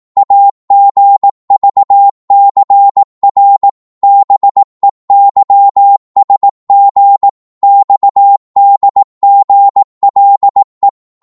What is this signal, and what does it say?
Morse code, heard as a keyed tone, about 18 wpm, 800 Hz. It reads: AGVCR BEYSG XDGLE
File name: Morse / ltr-18wpm